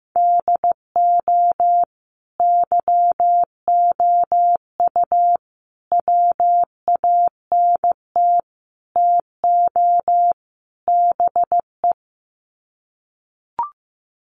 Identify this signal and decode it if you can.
Morse code, heard as a keyed tone, about 15 wpm, 700 Hz. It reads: DO YOU WANT TO BE
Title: Sets of 5 Words Encore2 15wpm